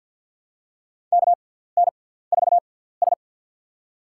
Morse code, heard as a keyed tone, about 51 wpm, spaced as for 17 wpm, 700 Hz. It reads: KN4S
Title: USA Call Signs - 6x Character Spacing - 50wpm